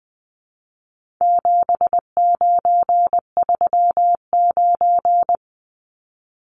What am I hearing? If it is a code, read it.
Morse code, heard as a keyed tone, about 20 wpm, 700 Hz. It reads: 7939
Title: Numbers - 4-Digit 20wpm